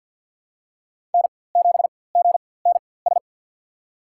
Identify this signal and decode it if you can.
Morse code, heard as a keyed tone, about 46 wpm, spaced as for 23 wpm, 700 Hz. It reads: N7GNS
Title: USA Call Signs - 4x Character Spacing - 50wpm